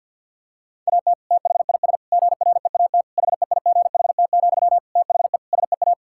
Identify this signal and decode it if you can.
Morse code, heard as a keyed tone, about 50 wpm, 700 Hz. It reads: AT THIS GREAT HEIGHT, THE SEA
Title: Sentences from Top 1600 Words 50wpm